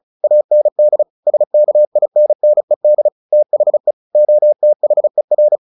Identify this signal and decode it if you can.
Morse code, heard as a keyed tone, about 34 wpm, 600 Hz. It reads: AND SKINNED THE OTHER
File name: the old man and the sea 2 35wpm